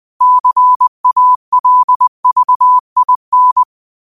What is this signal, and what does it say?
Morse code, heard as a keyed tone, about 20 wpm, 1000 Hz. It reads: CALVIN
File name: qotdfiles / 20wpm